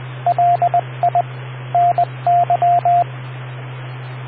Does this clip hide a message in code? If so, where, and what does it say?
Yes: Morse code from 0.3 to 3.0 s, heard as a keyed tone, about 20 wpm, 700 Hz, with radio noise. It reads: LI NY